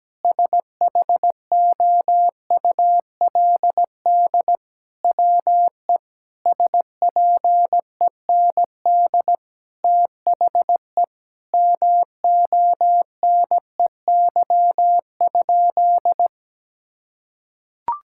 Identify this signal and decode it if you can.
Morse code, heard as a keyed tone, about 17 wpm, 700 Hz. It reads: SHOULD WE SPEND THE MONEY?
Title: Sentences from Top 1000 Words - Review Part 5 - 17wpm